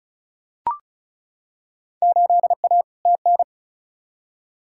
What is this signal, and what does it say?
Morse code, heard as a keyed tone, about 36 wpm, 700 Hz. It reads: 8A TN
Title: Field Day Exchanges Encore - 35wpm